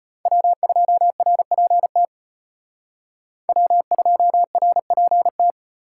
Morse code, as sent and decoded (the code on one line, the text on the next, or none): .-- ..--- .-. .--. - / .-- ..--- .-. .--. -
W2RPT W2RPT